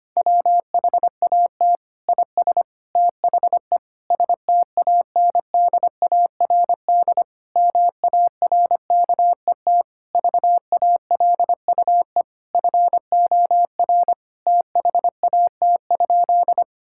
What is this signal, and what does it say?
Morse code, heard as a keyed tone, about 25 wpm, 700 Hz. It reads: WHAT IS THE STANDARD MARKET VALUE FOR THAT?
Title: Sentences from Top 1000 Words - Review Part 6 - 25wpm